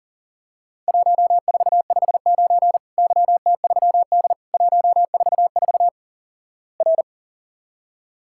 Morse code, heard as a keyed tone, about 38 wpm, 700 Hz. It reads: 1459 YT3D 144 R
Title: Worked All Europe DX Contest - QTC Exchange - 40wpm